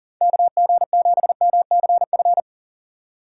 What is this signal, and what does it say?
Morse code, heard as a keyed tone, about 40 wpm, 700 Hz. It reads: KG7MCF
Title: USA Call Signs - 6 Characters - 4x Character Spacing - 40wpm